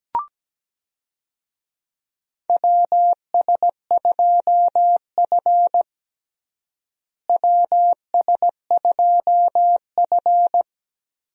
Morse code, heard as a keyed tone, about 17 wpm, 700 Hz. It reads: WS2F WS2F
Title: USA Call Signs - 4 Characters - Repeated 3x - 17wpm